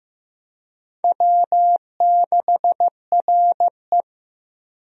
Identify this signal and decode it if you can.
Morse code, heard as a keyed tone, about 15 wpm, 700 Hz. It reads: W6RE
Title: USA Call Signs - 4 Characters - Repeated 3x - 15wpm